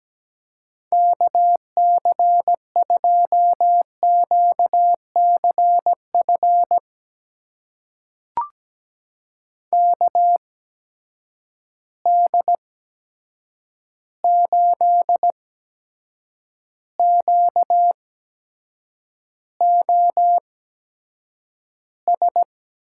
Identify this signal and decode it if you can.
Morse code, heard as a keyed tone, about 17 wpm, spaced as for 7 wpm, 700 Hz. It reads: KC2QCF KD8QOS
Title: USA Call Signs - 6 Characters - 8x Character Spacing - 17wpm